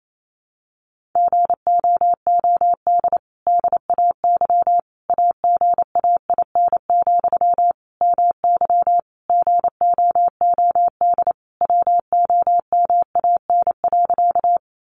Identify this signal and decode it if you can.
Morse code, heard as a keyed tone, about 28 wpm, 700 Hz. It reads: GOOD DAY AGAIN, MY GOOD WOMAN.
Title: Sentences from Top 1100 Words 28wpm